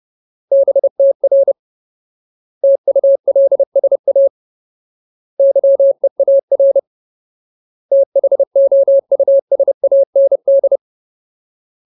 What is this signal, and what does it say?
Morse code, heard as a keyed tone, about 30 wpm, 550 Hz. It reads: BTR TULSA YEAR THOUSAND
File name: Ultimate Word List / 30WPM Ultimate